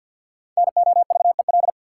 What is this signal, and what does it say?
Morse code, heard as a keyed tone, about 50 wpm, 700 Hz. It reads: NOVEL
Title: Sentences from Top 1300 Words - New Word Drill - 50wpm